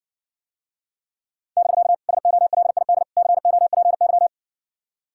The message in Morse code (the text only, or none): <BK> SOLID COPY